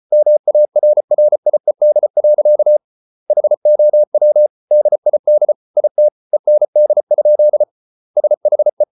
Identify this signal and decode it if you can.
Morse code, heard as a keyed tone, about 34 wpm, 600 Hz. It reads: MARRIED. HOW DID IT END? SHE